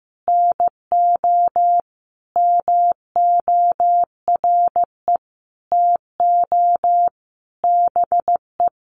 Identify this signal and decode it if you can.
Morse code, heard as a keyed tone, about 14 wpm, 700 Hz. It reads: NO MORE TO BE
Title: Sentences from Top 100 Words 15wpm